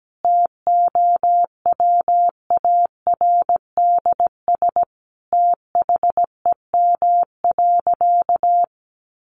Morse code, heard as a keyed tone, about 17 wpm, 700 Hz. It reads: TOWARDS THEM.
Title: Sentences from Top 1100 Words 17wpm